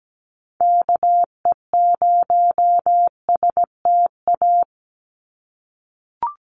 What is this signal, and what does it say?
Morse code, heard as a keyed tone, about 17 wpm, 700 Hz. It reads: KE0STA
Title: USA Call Signs 17wpm